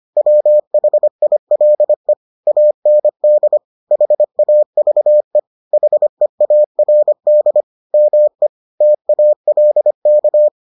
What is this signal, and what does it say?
Morse code, heard as a keyed tone, about 25 wpm, 600 Hz. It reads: WHILE AND HAVE HEARD ME TALK